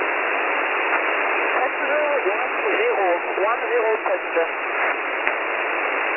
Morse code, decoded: none

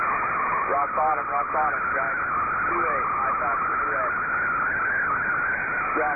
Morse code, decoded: none

